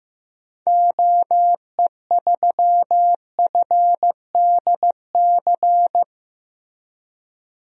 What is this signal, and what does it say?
Morse code, heard as a keyed tone, about 15 wpm, 700 Hz. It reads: OE3FDC
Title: DXCC-100 Call Signs - 6 Characters - Repeated 2x - 15wpm